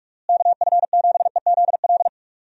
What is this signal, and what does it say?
Morse code, heard as a keyed tone, about 42 wpm, 700 Hz. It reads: KF7EZL
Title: USA Call Signs - 6 Characters - Repeated 2x - 45wpm